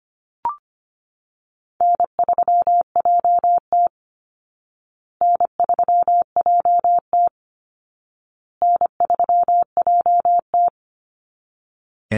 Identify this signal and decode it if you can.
Morse code, heard as a keyed tone, about 25 wpm, 700 Hz. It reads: N3JT N3JT N3JT
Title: Call Signs Repeated 2x - 25wpm